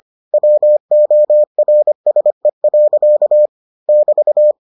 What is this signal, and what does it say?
Morse code, heard as a keyed tone, about 25 wpm, 600 Hz. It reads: WORSE. <BT>